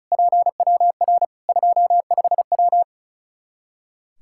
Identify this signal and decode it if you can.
Morse code, heard as a keyed tone, about 36 wpm, 700 Hz. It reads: PWR 25W
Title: Practice QSO - Exchange 2 - Info - 35wpm